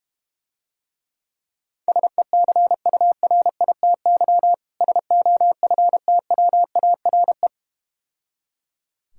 Morse code, heard as a keyed tone, about 32 wpm, 700 Hz. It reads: SECURITY SOFTWARE